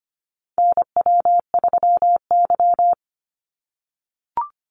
Morse code, heard as a keyed tone, about 25 wpm, 700 Hz. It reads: NW3Y
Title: Call Signs - 25wpm